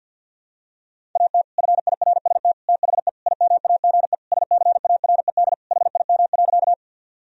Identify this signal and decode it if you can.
Morse code, heard as a keyed tone, about 50 wpm, 700 Hz. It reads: AT FIRST THE IMAGE SCARED HIM.
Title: Sentences from Top 1000 Words - Review Part 4 - 50wpm